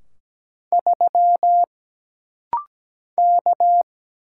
Morse code, heard as a keyed tone, about 17 wpm, 700 Hz. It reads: 3 K
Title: Single Letter-Number - Rapid-Fire 17wpm